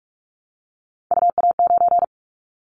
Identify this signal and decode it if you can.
Morse code, heard as a keyed tone, about 44 wpm, 700 Hz. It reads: UA9
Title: Callsign Prefixes for Top 100 DXCC - 45wpm